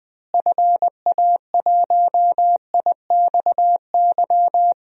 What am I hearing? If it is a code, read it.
Morse code, heard as a keyed tone, about 20 wpm, 700 Hz. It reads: FA1IXY
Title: DXCC-100 Call Signs - 6 Characters - Repeated 2x - 20wpm